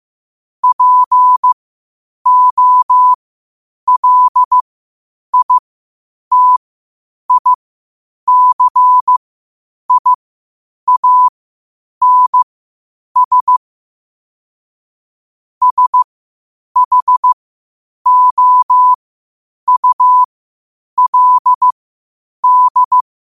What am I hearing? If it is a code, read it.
Morse code, heard as a keyed tone, about 15 wpm, spaced as for 8 wpm, 1000 Hz. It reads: POLITICIANS SHOULD
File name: qotdfiles / 8wpm